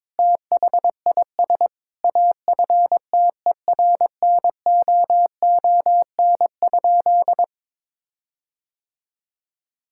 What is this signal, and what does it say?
Morse code, heard as a keyed tone, about 22 wpm, 700 Hz. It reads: THIS AFTERNOON?
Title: Sentences from Top 1600 Words 22wpm